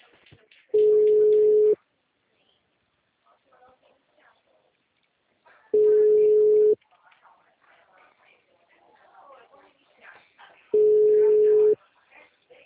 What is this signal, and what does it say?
Ringback tone (Europe, 425 Hz), with speech.